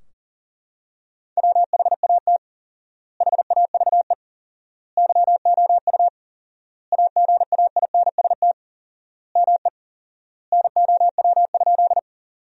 Morse code, heard as a keyed tone, about 40 wpm, 700 Hz. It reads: WHAT HAVE YOU AGAINST ME NOW?
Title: 4x Word-Spacing - Sentences from Top 400 Words - 40wpm